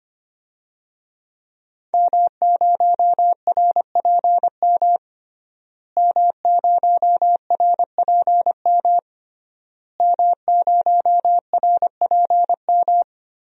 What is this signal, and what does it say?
Morse code, heard as a keyed tone, about 25 wpm, 700 Hz. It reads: M0RPM M0RPM M0RPM
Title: DXCC-100 Call Signs - 5 Characters - Repeated 3x - 25wpm